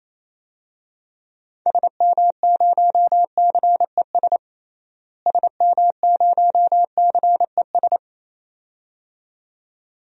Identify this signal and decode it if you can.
Morse code, heard as a keyed tone, about 28 wpm, 700 Hz. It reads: SM0CES SM0CES
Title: DXCC-100 Call Signs - 6 Characters - Repeated 2x - 28wpm